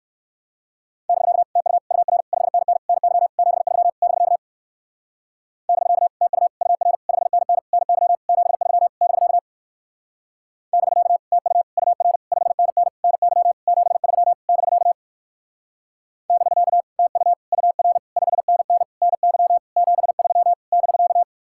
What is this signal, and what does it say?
Morse code, heard as a keyed tone, about 64 wpm, 700 Hz. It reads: <BK> TU UR 5NN NY 73 <BK> <BK> TU UR 5NN NY 73 <BK> <BK> TU UR 5NN NY 73 <BK> <BK> TU UR 5NN NY 73 <BK>